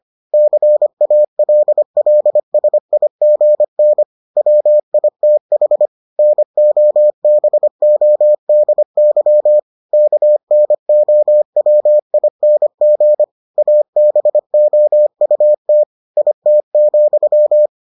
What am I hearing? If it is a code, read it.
Morse code, heard as a keyed tone, about 25 wpm, 600 Hz. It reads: CALLSIGN WITH NOBODY KNOWING ABOUT IT,